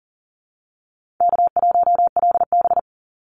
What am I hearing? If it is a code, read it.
Morse code, heard as a keyed tone, about 40 wpm, 700 Hz. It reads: K1LB
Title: NA SOTA Activator Callsigns 40wpm